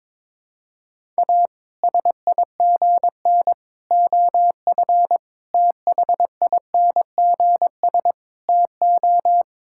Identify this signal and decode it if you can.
Morse code, heard as a keyed tone, about 22 wpm, 700 Hz. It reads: A SIGN OF THINGS TO
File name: practice / 22wpm / Sets of 6 Words - 22wpm